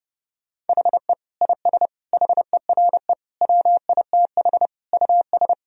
Morse code, heard as a keyed tone, about 30 wpm, 700 Hz. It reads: HE IS HERE WITH US
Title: Sentences from Top 1000 Words - Review Part 3 - 30wpm